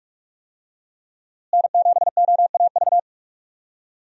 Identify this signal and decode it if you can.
Morse code, heard as a keyed tone, about 46 wpm, 700 Hz. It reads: N7OAV
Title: USA Call Signs - 5 Characters - Repeated 2x - 45wpm